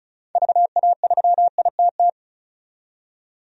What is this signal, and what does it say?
Morse code, heard as a keyed tone, about 36 wpm, 700 Hz. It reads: VA3ITT
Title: DXCC-100 Call Signs - 6 Characters - 35wpm